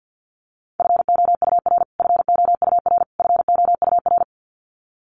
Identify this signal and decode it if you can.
Morse code, heard as a keyed tone, about 48 wpm, 700 Hz. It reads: FOUR FOUR FOUR
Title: Top 300 Words Repeated 50wpm